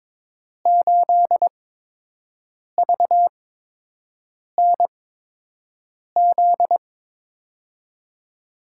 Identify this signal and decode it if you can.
Morse code, heard as a keyed tone, about 21 wpm, spaced as for 6 wpm, 700 Hz. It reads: 8VNZ
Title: USA Call Signs - 6 Characters - 8x Character Spacing - 22wpm